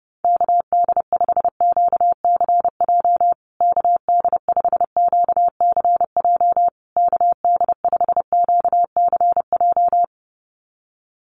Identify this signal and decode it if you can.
Morse code, heard as a keyed tone, about 30 wpm, 700 Hz. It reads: KD5QCJ KD5QCJ KD5QCJ
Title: USA Call Signs Repeated 3x 30wpm